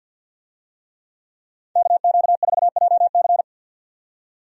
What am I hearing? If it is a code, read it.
Morse code, heard as a keyed tone, about 49 wpm, 700 Hz. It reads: KQ4JC
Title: USA Call Signs - 5 Characters - 50wpm